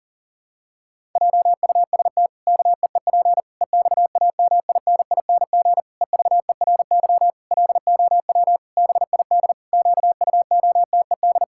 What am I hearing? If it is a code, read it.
Morse code, heard as a keyed tone, about 40 wpm, 700 Hz. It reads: JUST KEEP EXAMINING EVERY LOW BID QUOTED
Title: Pangram Challenge v2 - 40wpm